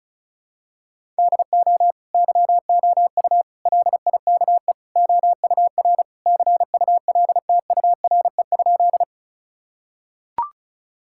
Morse code, heard as a keyed tone, about 36 wpm, 700 Hz. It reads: DO YOU LIKE OUR CULTURE?
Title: Sentences from Top 1000 Words - Review Part 6 - 35wpm